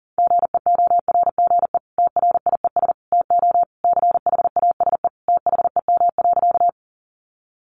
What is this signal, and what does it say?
Morse code, heard as a keyed tone, about 40 wpm, 700 Hz. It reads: GEORGE TRIES TO CHASE THEM.
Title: Additional Easy Phrases 40wpm